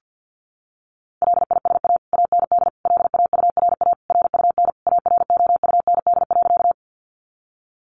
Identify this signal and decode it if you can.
Morse code, heard as a keyed tone, about 50 wpm, 700 Hz. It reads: LISA AND LAURA RUN AROUND.
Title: Additional Easy Phrases 50wpm